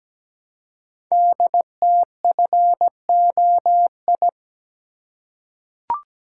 Morse code, heard as a keyed tone, about 17 wpm, 700 Hz. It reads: DTFOI
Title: Code Groups - 5 Letters - 17wpm